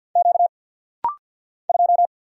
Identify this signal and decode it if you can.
Morse code, heard as a keyed tone, about 50 wpm, 700 Hz. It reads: Q 2
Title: Single Letter-Number Alt - Rapid-Fire 50wpm